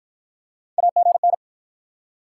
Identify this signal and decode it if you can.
Morse code, heard as a keyed tone, about 54 wpm, 700 Hz. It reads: AGN